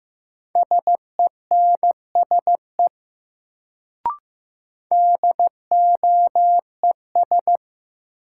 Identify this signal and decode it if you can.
Morse code, heard as a keyed tone, about 15 wpm, 700 Hz. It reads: SENSE DOES
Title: Sets of 2 Words - New Word Drill - 15wpm